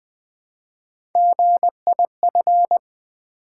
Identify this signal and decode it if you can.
Morse code, heard as a keyed tone, about 20 wpm, 700 Hz. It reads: GIF
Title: Build A Word - 1 Letter At A Time - 3 to 5-Letter Words - 20wpm